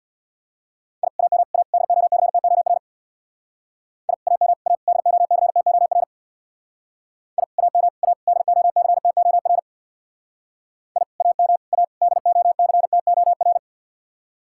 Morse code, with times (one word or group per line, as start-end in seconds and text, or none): 1.0-1.1 I
1.2-1.4 AM
1.5-1.6 A
1.7-2.8 DOCTOR
4.1-4.1 I
4.3-4.5 AM
4.7-4.8 A
4.9-6.0 DOCTOR
7.4-7.4 I
7.6-7.9 AM
8.0-8.1 A
8.3-9.6 DOCTOR
11.0-11.0 I
11.2-11.6 AM
11.7-11.8 A
12.0-13.6 DOCTOR